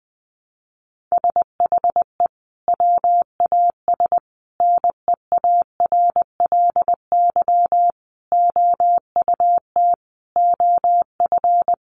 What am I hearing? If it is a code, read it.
Morse code, heard as a keyed tone, about 20 wpm, 700 Hz. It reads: SHE WAS NEARLY OUT OF
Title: Sentences from Top 1300 Words 20wpm